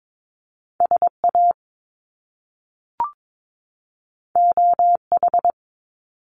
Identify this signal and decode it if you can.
Morse code, heard as a keyed tone, about 22 wpm, 700 Hz. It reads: SA OH
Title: Top Two Letter Combinations in Words 22wpm